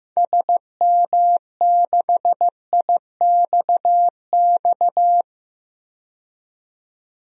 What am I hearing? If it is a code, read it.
Morse code, heard as a keyed tone, about 15 wpm, 700 Hz. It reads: SM6IXX
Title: DXCC-100 Call Signs - 6 Characters - 15wpm